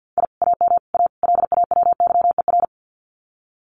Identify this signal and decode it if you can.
Morse code, heard as a keyed tone, about 50 wpm, 700 Hz. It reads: I AM A LAWYER
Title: Common QSO Phrases Encore 50wpm